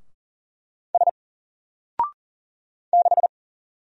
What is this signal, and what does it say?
Morse code, heard as a keyed tone, about 39 wpm, 700 Hz. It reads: S 6
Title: Single Character (Letter:Number:Punctuation:Prosign) 40wpm